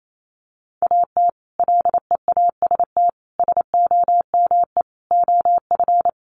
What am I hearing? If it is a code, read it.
Morse code, heard as a keyed tone, about 28 wpm, 690 Hz. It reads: AT LEAST SOME OF
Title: Sets of 5 Words Encore 28wpm